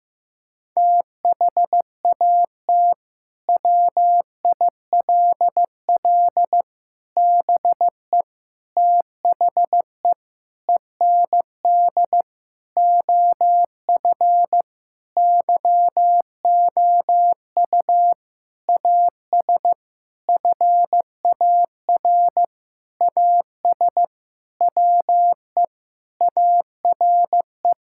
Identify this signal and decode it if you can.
Morse code, heard as a keyed tone, about 15 wpm, 700 Hz. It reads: THAT WILL BE THE END OF YOU AS FAR AS WE ARE
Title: Sentences from Top 1000 Words - Review Part 6 - 15wpm